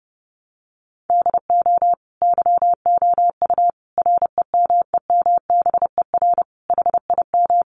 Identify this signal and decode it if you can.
Morse code, heard as a keyed tone, about 30 wpm, 700 Hz. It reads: DO YOU REMEMBER HIM
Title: Sentences from Top 1300 Words 30wpm